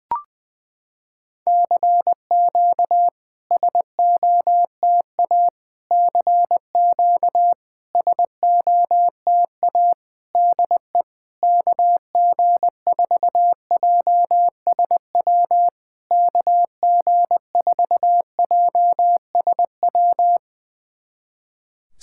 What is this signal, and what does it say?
Morse code, heard as a keyed tone, about 20 wpm, 700 Hz. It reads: CQ SOTA CQ SOTA DE KG4JSW KG4JSW
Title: SOTA QSO Practice 20wpm